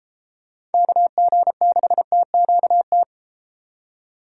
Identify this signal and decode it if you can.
Morse code, heard as a keyed tone, about 33 wpm, 700 Hz. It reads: KG6TQT